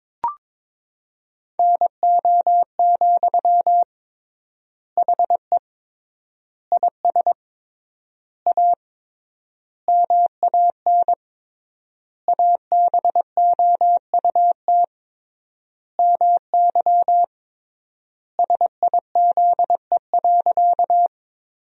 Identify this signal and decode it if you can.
Morse code, heard as a keyed tone, about 22 wpm, 700 Hz. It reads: NO, HE IS A MAN ABOUT MY SIZE.
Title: 3x Word-Spacing - Sentences from Top 400 Words - 22wpm